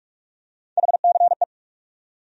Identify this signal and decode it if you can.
Morse code, heard as a keyed tone, about 46 wpm, 700 Hz. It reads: HCE
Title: USA Call Sign Postfixes - 3 Letters 45wpm